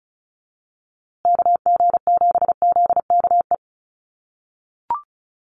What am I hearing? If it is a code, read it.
Morse code, heard as a keyed tone, about 34 wpm, 700 Hz. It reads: KG7ZKE
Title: USA Call Signs Repeated 3x 35wpm